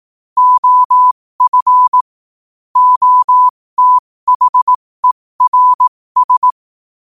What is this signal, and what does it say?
Morse code, heard as a keyed tone, about 18 wpm, spaced as for 15 wpm, 1000 Hz. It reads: OF OTHERS
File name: qotdfiles / 15wpm